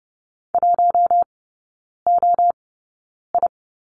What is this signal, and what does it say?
Morse code, heard as a keyed tone, about 30 wpm, spaced as for 9 wpm, 700 Hz. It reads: 1OI
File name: course / 30:10wpm / 087 Lesson 15e - Number 5 - Three Characters 30wpm